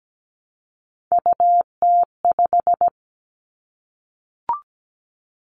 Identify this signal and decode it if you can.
Morse code, heard as a keyed tone, about 17 wpm, 700 Hz. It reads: UT5
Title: Callsign Prefixes for Top 25 DXCC - Repeated 3x - 17wpm